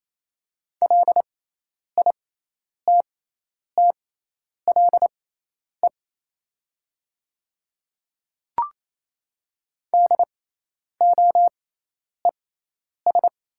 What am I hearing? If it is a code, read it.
Morse code, heard as a keyed tone, about 28 wpm, spaced as for 10 wpm, 700 Hz. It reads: LITTLE DOES
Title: 6x Character Spacing - Top 200 Words - 28wpm